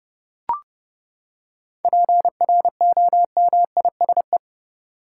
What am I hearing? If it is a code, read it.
Morse code, heard as a keyed tone, about 32 wpm, 700 Hz. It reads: PROMISE
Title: Seven Letter Words - 30wpm